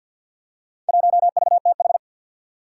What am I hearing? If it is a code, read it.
Morse code, heard as a keyed tone, about 47 wpm, 700 Hz. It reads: 1VTH
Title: Four Challenging Letter-Number Combinations 50wpm